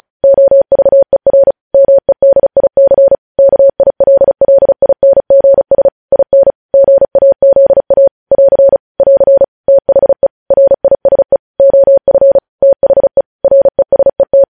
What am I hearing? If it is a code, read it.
Morse code, heard as a keyed tone, about 36 wpm, 550 Hz. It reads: OVER MEDIC KILLINGS IN GAZA <AR> <AR> THE RISE OF THE RESET
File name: podcast / News headlines-1x-35x35